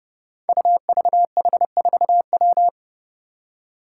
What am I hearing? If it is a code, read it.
Morse code, heard as a keyed tone, about 30 wpm, 700 Hz. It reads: UVH4W
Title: Five Challenging Letter-Number Combinations 30wpm